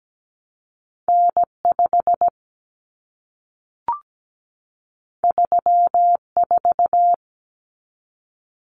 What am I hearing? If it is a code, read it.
Morse code, heard as a keyed tone, about 17 wpm, 700 Hz. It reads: N5 34
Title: Common Cut Numbers - 2-Digit 17wpm